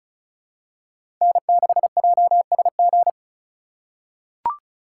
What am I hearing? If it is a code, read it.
Morse code, heard as a keyed tone, about 34 wpm, 700 Hz. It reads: N6JSG
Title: USA Call Signs - 5 Characters - 2x Character Spacing - 35wpm